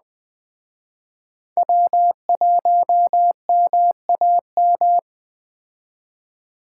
Morse code, heard as a keyed tone, about 20 wpm, 700 Hz. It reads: W1MAM
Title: Speed Racing - USA 1x3 Call Signs 17wpm